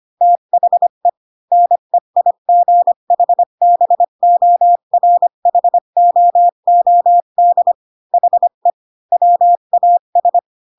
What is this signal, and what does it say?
Morse code, heard as a keyed tone, about 25 wpm, 700 Hz. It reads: THE NEIGHBORHOOD HE WAS
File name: Pinocchio / Pinocchio ch2 20 25 ik1hge